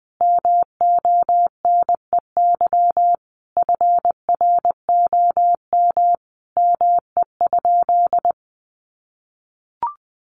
Morse code, heard as a keyed tone, about 20 wpm, 700 Hz. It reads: MONEY FROM ME?
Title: Sentences from Top 1500 Words 20wpm